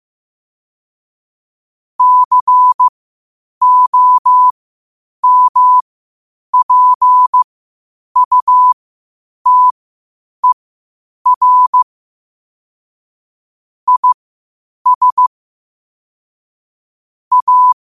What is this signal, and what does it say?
Morse code, heard as a keyed tone, about 15 wpm, spaced as for 8 wpm, 1000 Hz. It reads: COMPUTER IS A